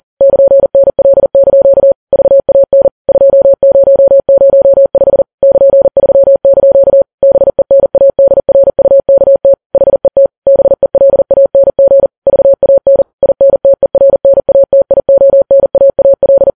Morse code, heard as a keyed tone, about 40 wpm, 550 Hz. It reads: <KN>NL) VAN 2005 <KN>3) BENADRUKT HET BELANG VAN INTERNATIONAAL
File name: Files Web tls / Morse on9fv / morse40kps